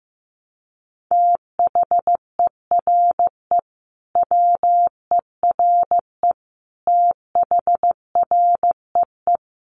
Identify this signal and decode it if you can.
Morse code, heard as a keyed tone, about 15 wpm, 700 Hz. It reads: THERE WERE THREE
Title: Sentences from Top 800 Words 15wpm